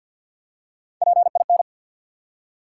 Morse code, heard as a keyed tone, about 54 wpm, 700 Hz. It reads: PIN
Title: Three Letter Words - 50wpm